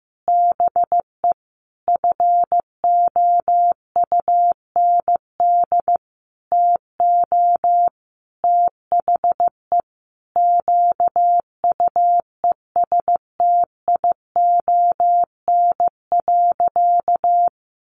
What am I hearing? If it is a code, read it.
Morse code, heard as a keyed tone, about 15 wpm, 700 Hz. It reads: BE FOUND TO THE QUESTION.
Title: Sentences from Top 1400 Words 15wpm